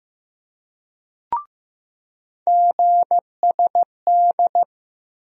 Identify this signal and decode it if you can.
Morse code, heard as a keyed tone, about 15 wpm, 700 Hz. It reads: GSD
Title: USA Call Sign Postfixes - 3 Letters 15wpm